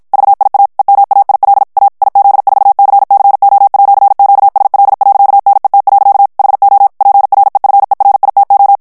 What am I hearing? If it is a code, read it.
Morse code, heard as a keyed tone, about 53 wpm, 800 Hz. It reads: VIA EMAIL A IZ4CCO@QSL.NET. HO PREFERITO
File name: ITALIEN texte clair 1 vitesse 50 mots minute